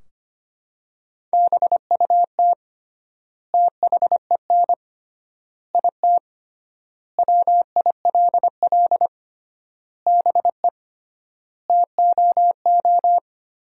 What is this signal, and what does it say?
Morse code, heard as a keyed tone, about 25 wpm, 700 Hz. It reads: BUT THEN IT WILL BE TOO
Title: 3x Word-Spacing - Sentences from Top 300 Words - 25wpm